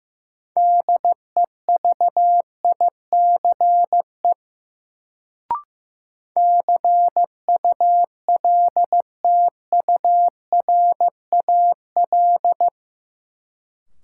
Morse code, heard as a keyed tone, about 15 wpm, 700 Hz. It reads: DEVICE CULTURAL
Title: Sentences from Top 1500 Words - New Word Drill - 15wpm